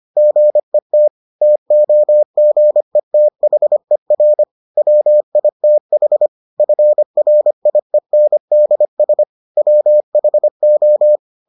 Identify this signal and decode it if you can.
Morse code, heard as a keyed tone, about 25 wpm, 600 Hz. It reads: GET TOGETHER WITH FRIENDS WHO